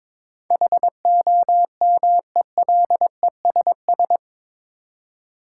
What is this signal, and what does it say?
Morse code, heard as a keyed tone, about 22 wpm, 700 Hz. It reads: HOMELESS
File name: practice / 22wpm / Eight Letter Words - 22wpm